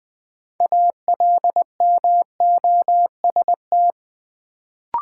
Morse code, heard as a keyed tone, about 20 wpm, 700 Hz. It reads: ALMOST
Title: Sentences from Top 700 Words - New Word Drill - 20wpm